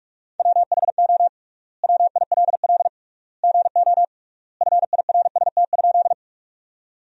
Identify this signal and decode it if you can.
Morse code, heard as a keyed tone, about 45 wpm, 700 Hz. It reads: WHO WILL GO FIRST?
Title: 3x Word-Spacing - Sentences from Top 100 Words - 45wpm